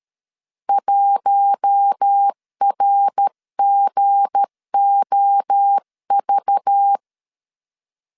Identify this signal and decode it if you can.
Morse code, heard as a keyed tone, about 13 wpm, 800 Hz. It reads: 1RGOV
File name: LEZIONI / 7 GF9Z / 7 G 12wpm 2055